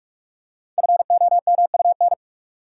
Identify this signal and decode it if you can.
Morse code, heard as a keyed tone, about 44 wpm, 700 Hz. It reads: FOMUN
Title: Code Groups - 5 Letters - 45wpm